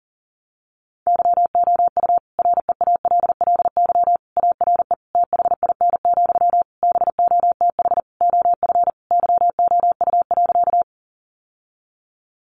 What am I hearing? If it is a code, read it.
Morse code, heard as a keyed tone, about 40 wpm, 700 Hz. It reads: YOU REALLY ARE THIN, BOTH OF YOU.
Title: Sentences from Top 1200 Words 40wpm